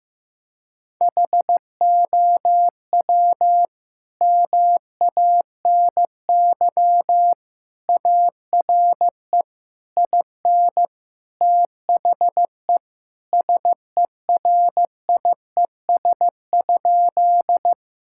Morse code, heard as a keyed tone, about 15 wpm, 700 Hz. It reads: HOW MANY ARE IN THE SERIES?
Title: Sentences from Top 1000 Words - Review Part 5 - 15wpm